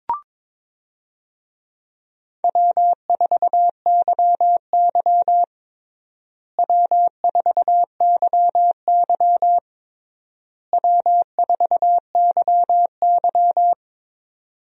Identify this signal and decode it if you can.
Morse code, heard as a keyed tone, about 21 wpm, 700 Hz. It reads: W4YY W4YY W4YY
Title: USA Call Signs - 4 Characters - Repeated 3x - 22wpm